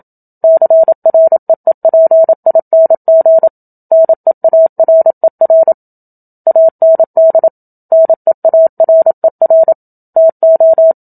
Morse code, heard as a keyed tone, about 27 wpm, 650 Hz. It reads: CREEPING NEARER AND NEARER TO
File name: Book Two / 01-Chapter 1 - 27 WPM